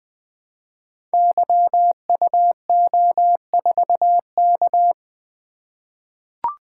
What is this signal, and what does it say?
Morse code, heard as a keyed tone, about 20 wpm, 700 Hz. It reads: YUO4K
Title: Code Groups - 5 Letter:Numbers - 20wpm